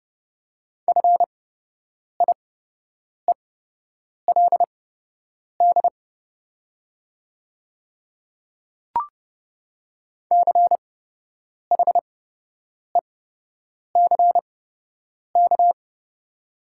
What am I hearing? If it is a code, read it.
Morse code, heard as a keyed tone, about 30 wpm, spaced as for 8 wpm, 700 Hz. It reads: FIELD CHECK
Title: 8x Character Spacing - Top 500 Words - 30wpm